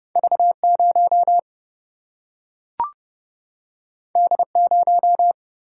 Morse code, heard as a keyed tone, about 30 wpm, 700 Hz. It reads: V0 D0
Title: Single-Letter Single-Number 30wpm